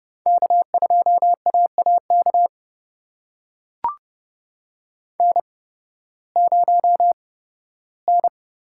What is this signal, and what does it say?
Morse code, heard as a keyed tone, about 30 wpm, spaced as for 16 wpm, 700 Hz. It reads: K2AAK N0N
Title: USA Call Signs - 5 Characters - 8x Character Spacing - 30wpm